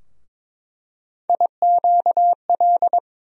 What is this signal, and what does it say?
Morse code, heard as a keyed tone, about 22 wpm, 700 Hz. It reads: IQL
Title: USA Call Sign Postfixes - 3 Letters 22wpm